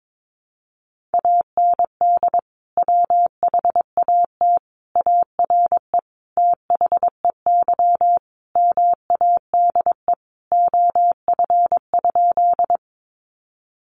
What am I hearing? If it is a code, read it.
Morse code, heard as a keyed tone, about 22 wpm, 700 Hz. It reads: AND WHAT ARE THEY MADE OF?
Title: Sentences from Top 200 Words 22wpm